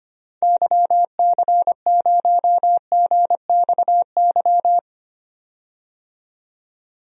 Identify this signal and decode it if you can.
Morse code, heard as a keyed tone, about 25 wpm, 700 Hz. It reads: YC0GXY
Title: DXCC-100 Call Signs - 6 Characters - 25wpm